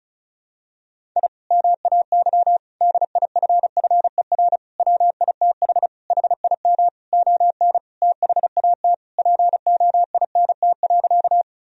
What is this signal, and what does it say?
Morse code, heard as a keyed tone, about 34 wpm, 700 Hz. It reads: I MAY DIFFER WITH HIM ON THAT POINT.
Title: Sentences from Top 1000 Words - Review Part 1 - 35wpm